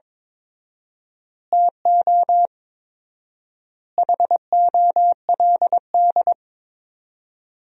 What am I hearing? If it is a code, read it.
Morse code, heard as a keyed tone, about 22 wpm, 700 Hz. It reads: TO HOLD
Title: 4x Word-Spacing - Sentences from Top 400 Words - 22wpm